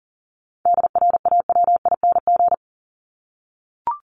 Morse code, heard as a keyed tone, about 40 wpm, 700 Hz. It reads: DRAWING
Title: Top 1751-2000 Words 40wpm